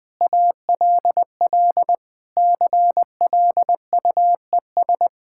Morse code, heard as a keyed tone, about 20 wpm, 690 Hz. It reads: ALL CLUES
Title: Sentences with 4-Word Phrases - Spoken Beginning - 20wpm